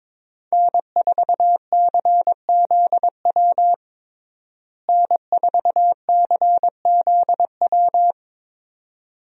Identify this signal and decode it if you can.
Morse code, heard as a keyed tone, about 22 wpm, 700 Hz. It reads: N4CZW N4CZW
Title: USA Call Signs - 5 Characters - Repeated 2x - 22wpm